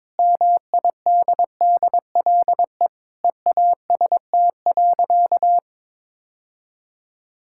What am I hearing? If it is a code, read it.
Morse code, heard as a keyed tone, about 22 wpm, 700 Hz. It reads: MIDDLE EAST.
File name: practice / 22wpm / Sentences with 3-Word Phrases - 22wpm